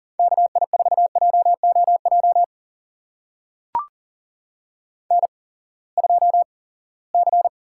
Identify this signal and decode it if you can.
Morse code, heard as a keyed tone, about 39 wpm, spaced as for 22 wpm, 700 Hz. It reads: KI4JOJ N2C
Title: USA Call Signs - 8x Character Spacing - 40wpm